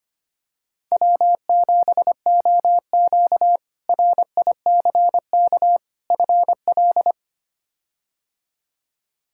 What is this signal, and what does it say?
Morse code, heard as a keyed tone, about 25 wpm, 700 Hz. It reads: W7OQ RICK FL
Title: K1USN SST QSO Practice - 25wpm